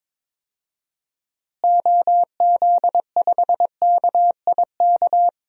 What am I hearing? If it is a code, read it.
Morse code, heard as a keyed tone, about 22 wpm, 700 Hz. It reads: OZ5KIK
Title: DXCC-100 Call Signs - 6 Characters - Repeated 2x - 22wpm